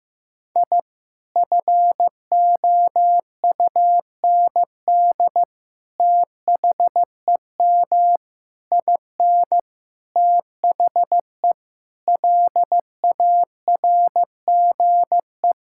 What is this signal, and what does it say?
Morse code, heard as a keyed tone, about 15 wpm, 700 Hz. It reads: I FOUND THEM IN THE LARGE
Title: Sentences from Top 1000 Words - Review Part 5 - 15wpm